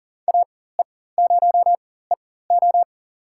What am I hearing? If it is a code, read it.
Morse code, heard as a keyed tone, about 40 wpm, spaced as for 19 wpm, 700 Hz. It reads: AE0EO
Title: USA Call Signs - 5 Characters - 4x Character Spacing - 40wpm